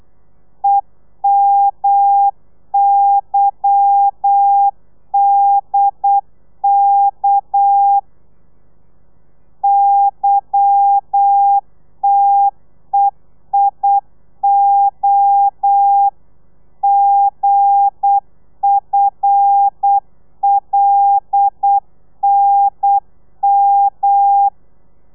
Morse code, heard as a keyed tone, about 8 wpm, 800 Hz. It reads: EMYDK YTEIO GFLNM
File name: telegrafia / i2tfj / Lez11 35